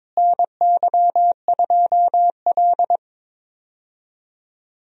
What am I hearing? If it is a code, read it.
Morse code, heard as a keyed tone, about 22 wpm, 700 Hz. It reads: NY2L
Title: USA Call Signs - 4 Characters - Repeated 3x - 22wpm